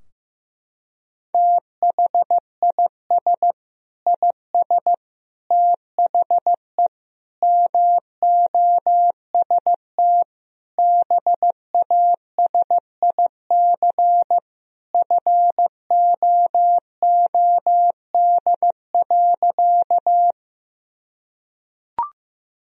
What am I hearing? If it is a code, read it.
Morse code, heard as a keyed tone, about 15 wpm, 700 Hz. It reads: THIS IS THE MOST BASIC FOOD.
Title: Sentences from Top 1000 Words - Review Part 3 - 15wpm